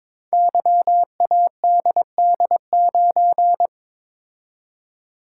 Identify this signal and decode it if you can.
Morse code, heard as a keyed tone, about 22 wpm, 700 Hz. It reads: YADD9
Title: Code Groups - 5 Letter:Numbers - 22wpm